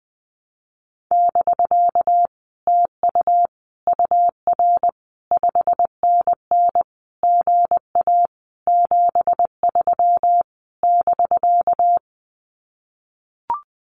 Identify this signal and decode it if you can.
Morse code, heard as a keyed tone, about 20 wpm, 700 Hz. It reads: <BK> TU UR 5NN GA 73 <BK>
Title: Common QSO Phrases Encore 20wpm